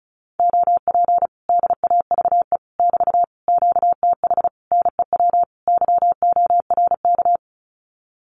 Morse code, heard as a keyed tone, about 34 wpm, 700 Hz. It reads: OP DAVE <BT> QTH NEW YORK
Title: Common QSO Phrases Encore 35wpm